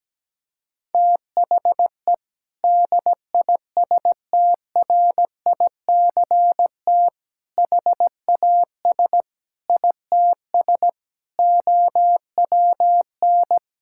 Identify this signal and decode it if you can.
Morse code, heard as a keyed tone, about 17 wpm, 700 Hz. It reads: THE DISTRICT HAS ITS OWN
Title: Sentences from Top 1600 Words 17wpm